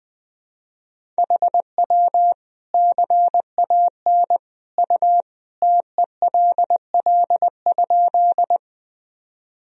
Morse code, heard as a keyed tone, about 21 wpm, 700 Hz. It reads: HW CAN U TELL?
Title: Sentences from Top 200 Words and Abbreviations 20wpm